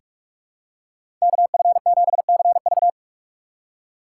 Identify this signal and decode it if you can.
Morse code, heard as a keyed tone, about 42 wpm, 700 Hz. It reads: KF7CV
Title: USA Call Signs - 5 Characters - 3x Character Spacing - 45wpm